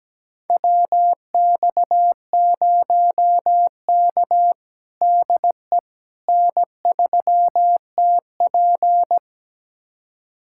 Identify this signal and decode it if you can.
Morse code, heard as a keyed tone, about 17 wpm, 700 Hz. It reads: WX0K DE N3TP
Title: USA Callsign DE Callsign - 4 Characters - 17wpm